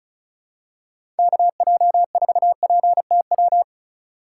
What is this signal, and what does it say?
Morse code, heard as a keyed tone, about 36 wpm, 700 Hz. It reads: KJ4PTW
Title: USA Call Signs - 6 Characters - 4x Character Spacing - 35wpm